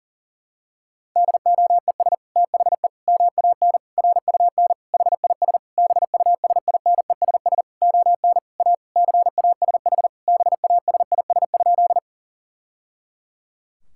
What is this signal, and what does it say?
Morse code, heard as a keyed tone, about 40 wpm, 700 Hz. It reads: DOES THE MAN RUN HIS BUSINESS ON A CASH BASIS?
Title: Sentences from Top 1000 Words - Review Part 5 - 40wpm